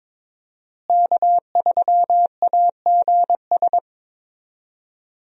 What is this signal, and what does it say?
Morse code, heard as a keyed tone, about 22 wpm, 700 Hz. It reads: K3AGS
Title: USA Call Signs - 5 Characters - Repeated 3x - 22wpm